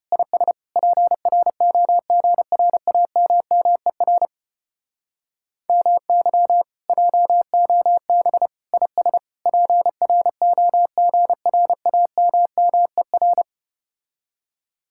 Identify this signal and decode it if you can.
Morse code, heard as a keyed tone, about 32 wpm, 700 Hz. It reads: IS PROGRAMMER MY JOB IS PROGRAMMER